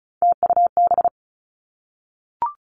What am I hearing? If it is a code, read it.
Morse code, heard as a keyed tone, about 36 wpm, 700 Hz. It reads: TUB
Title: Three Challenging Letter-Number Combinations 35wpm